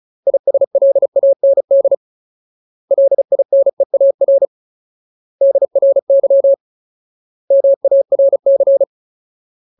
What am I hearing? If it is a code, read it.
Morse code, heard as a keyed tone, about 34 wpm, 550 Hz. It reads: ISLAND LINEAR DRY MARC
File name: Ultimate Word List / 35WPM Ultimate